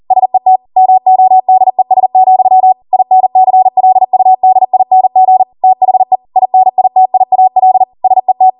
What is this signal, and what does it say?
Morse code, heard as a keyed tone, about 40 wpm, 750 Hz. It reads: SET MODES, INCLUDING THE INITIAL SET
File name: Morse / 40WPM